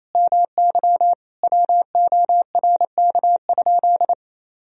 Morse code, heard as a keyed tone, about 28 wpm, 700 Hz. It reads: MY WORK?
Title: Sentences from Top 500 Words and Abbreviations 28wpm